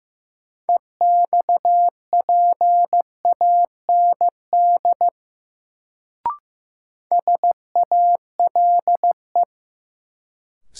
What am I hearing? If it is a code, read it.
Morse code, heard as a keyed tone, about 15 wpm, 700 Hz. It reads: EXPAND SALE
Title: Sentences from Top 1400 Words - New Word Drill - 15wpm